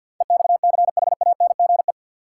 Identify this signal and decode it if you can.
Morse code, heard as a keyed tone, about 47 wpm, 700 Hz. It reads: EXCHANGE
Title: Sets of 5 Words - New Word Drill - 50wpm